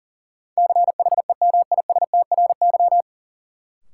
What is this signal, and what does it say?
Morse code, heard as a keyed tone, about 38 wpm, 700 Hz. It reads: CHEMISTRY
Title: Sentences from Top 1000 Words - Review Part 6 - New Word Drill - 40wpm